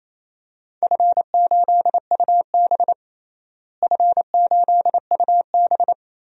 Morse code, heard as a keyed tone, about 28 wpm, 700 Hz. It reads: F8UB F8UB
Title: DXCC-100 Call Signs - 4 Characters - Repeated 2x - 28wpm